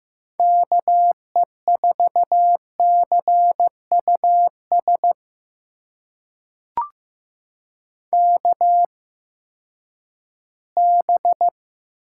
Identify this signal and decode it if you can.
Morse code, heard as a keyed tone, about 15 wpm, spaced as for 10 wpm, 700 Hz. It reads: KE4CUS KB
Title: USA Call Signs - 6 Characters - 8x Character Spacing - 15wpm